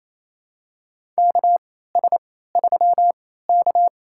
Morse code, heard as a keyed tone, about 28 wpm, spaced as for 16 wpm, 700 Hz. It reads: KS3K
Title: USA Call Signs - 4 Characters - 3x Character Spacing - 28wpm